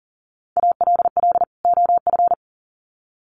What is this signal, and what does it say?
Morse code, heard as a keyed tone, about 40 wpm, 700 Hz. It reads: ALL OF
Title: Sets of 2 Words 40wpm